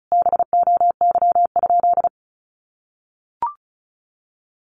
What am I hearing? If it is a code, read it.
Morse code, heard as a keyed tone, about 34 wpm, 700 Hz. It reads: BOY?
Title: Sentences from Top 1500 Words 35wpm